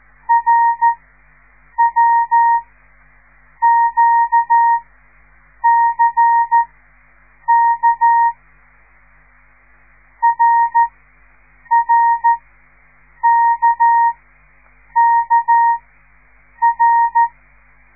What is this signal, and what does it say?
Morse code, heard as a keyed tone, about 13 wpm, spaced as for 7 wpm, 930 Hz. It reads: RWQCK RRKKR
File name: radiomorningstar / morse / les 5